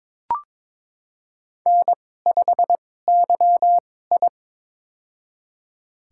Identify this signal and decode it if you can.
Morse code, heard as a keyed tone, about 22 wpm, spaced as for 16 wpm, 700 Hz. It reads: N5YI
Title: USA Call Signs - 4 Characters - 2x Character Spacing - 22wpm